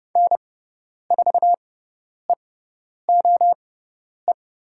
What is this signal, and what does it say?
Morse code, heard as a keyed tone, about 30 wpm, spaced as for 10 wpm, 700 Hz. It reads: N4EOE
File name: Lesson 17g - Letter u - Call Signs 30wpm